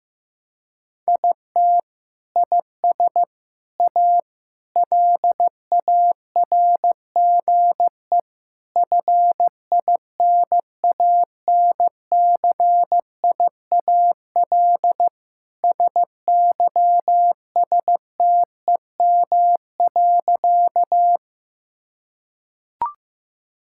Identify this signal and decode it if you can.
Morse code, heard as a keyed tone, about 15 wpm, 700 Hz. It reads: IT IS A LARGE FINANCIAL SYSTEM.
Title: Sentences from Top 1000 Words - Review Part 5 - 15wpm